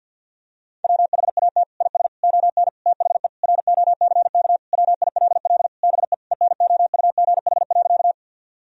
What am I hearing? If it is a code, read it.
Morse code, heard as a keyed tone, about 50 wpm, 700 Hz. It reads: WHAT IS ON THE ROCK WILL BE ENOUGH.
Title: Sentences from Top 1000 Words - Review Part 2 - 50wpm